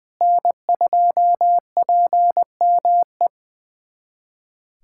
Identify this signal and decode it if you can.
Morse code, heard as a keyed tone, about 19 wpm, 700 Hz. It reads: N2PME
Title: USA Callsign DE Callsign - 5 Characters - 20wpm